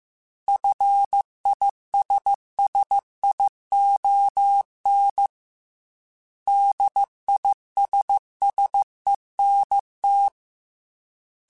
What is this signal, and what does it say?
Morse code, heard as a keyed tone, about 15 wpm, 800 Hz. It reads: FISSION DISSENT